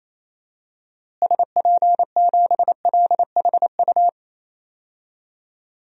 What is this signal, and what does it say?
Morse code, heard as a keyed tone, about 28 wpm, 700 Hz. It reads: SP7LHU
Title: DXCC-100 Call Signs - 6 Characters - Repeated 3x - 28wpm